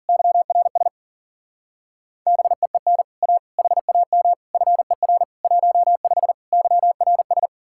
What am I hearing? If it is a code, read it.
Morse code, heard as a keyed tone, about 50 wpm, 700 Hz. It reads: YRS BEEN A HAM FER 15 YRS
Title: Speed-Racing Common QSO Phrases - 40wpm